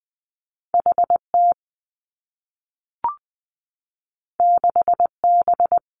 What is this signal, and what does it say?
Morse code, heard as a keyed tone, about 20 wpm, 700 Hz. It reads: H T 6 B
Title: Two Challenging Letter-Number Combinations 20wpm